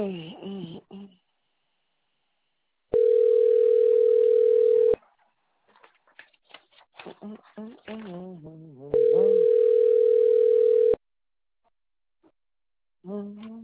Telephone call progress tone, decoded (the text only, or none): ringback tone